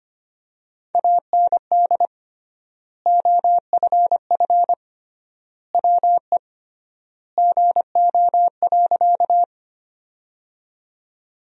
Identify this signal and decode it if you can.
Morse code, heard as a keyed tone, about 25 wpm, 700 Hz. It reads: AND OFF WE GO.
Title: 3x Word-Spacing - Sentences from Top 200 Words - 25wpm